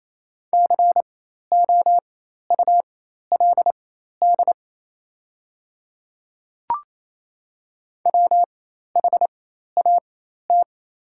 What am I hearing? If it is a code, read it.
Morse code, heard as a keyed tone, about 27 wpm, spaced as for 13 wpm, 690 Hz. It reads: COULD WHAT
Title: 4x Character Spacing - Top 100 Words - 28wpm